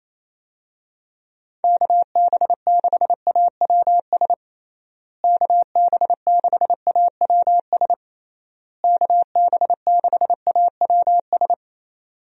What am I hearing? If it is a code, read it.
Morse code, heard as a keyed tone, about 28 wpm, 700 Hz. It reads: KB6AWS KB6AWS KB6AWS
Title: USA Call Signs - 6 Characters - Repeated 3x - 28wpm